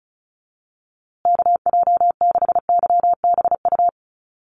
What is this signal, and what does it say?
Morse code, heard as a keyed tone, about 34 wpm, 700 Hz. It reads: KJ6YBU
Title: USA Call Signs Repeated 2x 35wpm